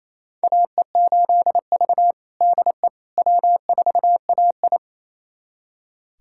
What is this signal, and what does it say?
Morse code, heard as a keyed tone, about 29 wpm, 690 Hz. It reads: AE8V DE W4AI
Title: USA Callsign DE Callsign - 4 Characters - 28wpm